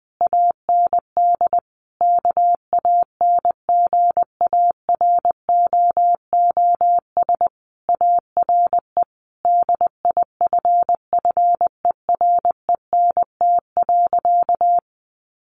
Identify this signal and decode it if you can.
Morse code, heard as a keyed tone, about 19 wpm, 700 Hz. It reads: AND KANGAROOS ARE DIFFERENT.
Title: Easy Phrases 20wpm